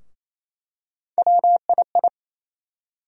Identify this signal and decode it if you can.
Morse code, heard as a keyed tone, about 28 wpm, 700 Hz. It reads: WII
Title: USA Call Sign Postfixes - 3 Letters 28wpm